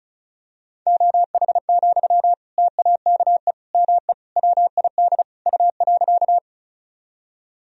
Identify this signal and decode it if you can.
Morse code, heard as a keyed tone, about 34 wpm, 700 Hz. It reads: OH, TAKE ME WID U.
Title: Sentences from Top 500 Words and Abbreviations 35wpm